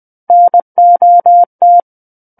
Morse code, heard as a keyed tone, about 20 wpm, 700 Hz. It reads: NOT
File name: 20 WPM